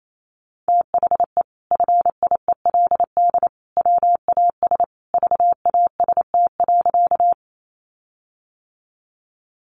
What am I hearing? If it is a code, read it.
Morse code, heard as a keyed tone, about 28 wpm, 700 Hz. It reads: THE FIELD WAS VAST.
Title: Sentences from Top 1300 Words 28wpm